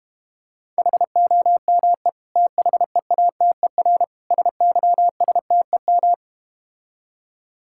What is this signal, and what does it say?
Morse code, heard as a keyed tone, about 32 wpm, 690 Hz. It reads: HOME THEATER SYSTEM